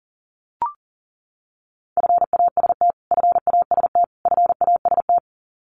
Morse code, heard as a keyed tone, about 40 wpm, 700 Hz. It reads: FAST FAST FAST
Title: Top 500 Words Repeated 40wpm